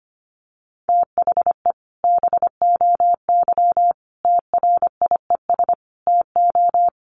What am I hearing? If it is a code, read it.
Morse code, heard as a keyed tone, about 25 wpm, 700 Hz. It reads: THE BOY TRIES TO
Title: Additional Easy Phrases 25wpm